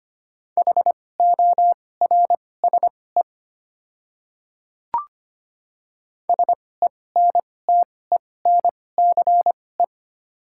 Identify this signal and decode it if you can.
Morse code, heard as a keyed tone, about 25 wpm, spaced as for 18 wpm, 700 Hz. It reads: HORSE SENTENCE
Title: 2x Character Spacing - Top 300 Words - 25wpm